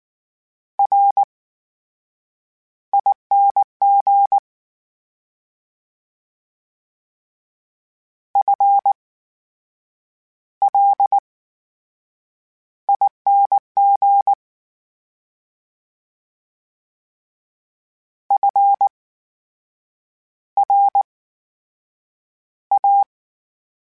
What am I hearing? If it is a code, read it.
Morse code, heard as a keyed tone, about 19 wpm, spaced as for 7 wpm, 800 Hz. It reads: RING FLING FRA
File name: Special INGs / Special INGs-mp3s / VerySlowToFaster Ing @18